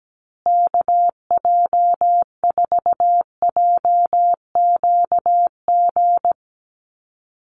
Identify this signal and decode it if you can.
Morse code, heard as a keyed tone, about 17 wpm, 700 Hz. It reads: KJ4JQG
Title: USA Call Signs Repeated 3x 17wpm